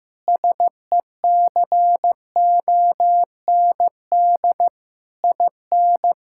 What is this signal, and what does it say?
Morse code, heard as a keyed tone, about 15 wpm, 700 Hz. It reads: SECOND IN
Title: Sentences from Top 1600 Words 15wpm